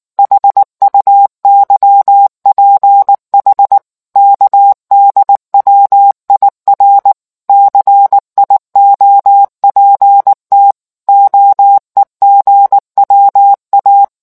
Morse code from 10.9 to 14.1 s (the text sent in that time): OEGWA